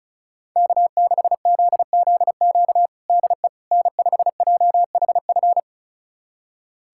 Morse code, heard as a keyed tone, about 36 wpm, 700 Hz. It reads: K6ZZQ DE N5JHF
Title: USA Callsign DE Callsign - 5 Characters - 35wpm